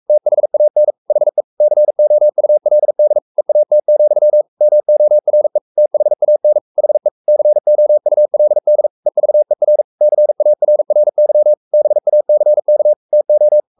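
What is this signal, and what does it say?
Morse code, heard as a keyed tone, about 44 wpm, 600 Hz. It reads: THAN HE COULD EAT, MORE THAN HE COULD EVER CARRY BACK TO
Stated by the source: Morse code at 43 wpm sending Ken Follett's novel Never in English, from CW med Gnister.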